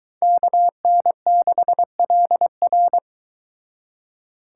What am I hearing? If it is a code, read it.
Morse code, heard as a keyed tone, about 22 wpm, 700 Hz. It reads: KN6LR